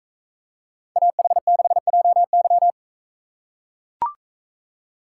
Morse code, heard as a keyed tone, about 42 wpm, 700 Hz. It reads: AH6JY